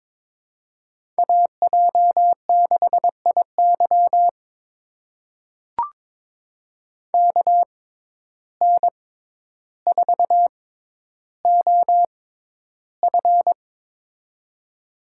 Morse code, heard as a keyed tone, about 21 wpm, spaced as for 11 wpm, 700 Hz. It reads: AJ6IY KN4OF
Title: USA Call Signs - 5 Characters - 6x Character Spacing - 22wpm